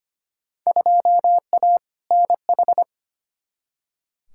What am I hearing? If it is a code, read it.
Morse code, heard as a keyed tone, about 25 wpm, 700 Hz. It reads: 2A NH
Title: Field Day Exchanges - 25wpm